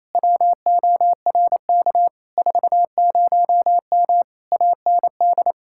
Morse code, heard as a keyed tone, about 27 wpm, 700 Hz. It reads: WORK 40M AND
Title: Speed-Racing Common QSO Phrases Encore - 28wpm